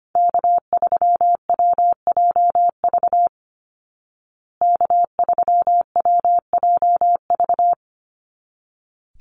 Morse code, heard as a keyed tone, about 25 wpm, 700 Hz. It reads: K3WJV K3WJV
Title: Call Signs Repeated 3x - 25wpm